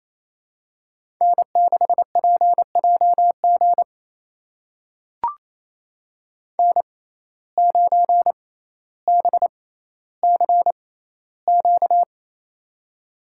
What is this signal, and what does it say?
Morse code, heard as a keyed tone, about 27 wpm, spaced as for 14 wpm, 700 Hz. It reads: N6PJG N9BCQ
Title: USA Call Signs - 5 Characters - 6x Character Spacing - 28wpm